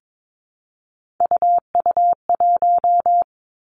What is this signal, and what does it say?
Morse code, heard as a keyed tone, about 22 wpm, 700 Hz. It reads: UU1
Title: Three Challenging Letter-Number Combinations 22wpm